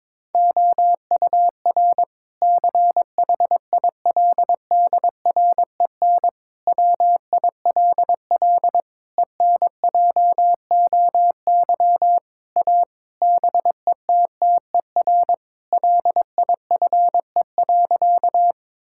Morse code, heard as a keyed tone, about 22 wpm, 700 Hz. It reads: OUR CHILDREN WILL ENJOY A BETTER LIFE.
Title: Sentences from Top 1000 Words - Review Part 5 - 22wpm